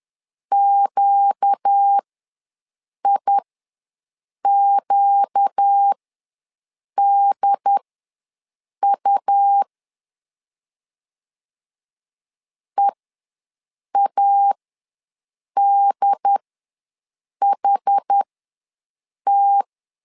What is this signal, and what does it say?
Morse code, heard as a keyed tone, about 10 wpm, spaced as for 6 wpm, 800 Hz. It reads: QIQDU EADHT
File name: LEZIONI / 4 Q ws 10wpm 2306